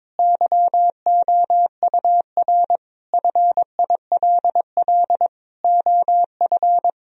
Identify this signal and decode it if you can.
Morse code, heard as a keyed tone, about 22 wpm, 700 Hz. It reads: YOUR FILL OF
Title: Sentences from Top 1000 Words - Review Part 3 - 22wpm